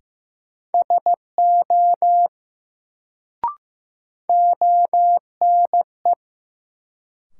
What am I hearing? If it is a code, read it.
Morse code, heard as a keyed tone, about 15 wpm, 700 Hz. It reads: SO ONE
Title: Sets of 2 Words - New Word Drill - 15wpm